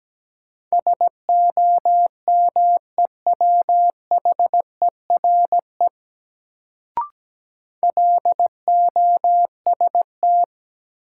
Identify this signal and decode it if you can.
Morse code, heard as a keyed tone, about 17 wpm, 700 Hz. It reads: SOMEWHERE LOST
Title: Sentences from Top 1400 Words - New Word Drill - 17wpm